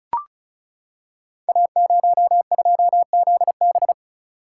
Morse code, heard as a keyed tone, about 36 wpm, 700 Hz. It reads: A02ZB
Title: Code Groups - 5 Letter:Numbers - 35wpm